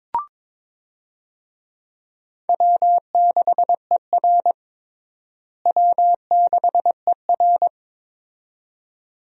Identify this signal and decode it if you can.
Morse code, heard as a keyed tone, about 22 wpm, 700 Hz. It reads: W6ER W6ER
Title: USA Call Signs - 4 Characters - Repeated 2x - 22wpm